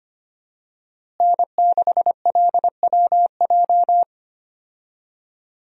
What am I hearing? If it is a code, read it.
Morse code, heard as a keyed tone, about 25 wpm, 700 Hz. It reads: N6LWJ
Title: USA Call Signs - 5 Characters - Repeated 3x - 25wpm